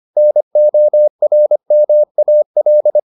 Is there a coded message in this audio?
Yes: Morse code from 0.2 to 3.0 s, heard as a keyed tone, about 25 wpm, 600 Hz. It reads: NORMAL